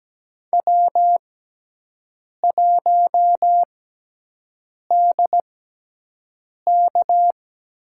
Morse code, heard as a keyed tone, about 17 wpm, spaced as for 6 wpm, 700 Hz. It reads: W1DK
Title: USA Call Signs - 6x Character Spacing - 17wpm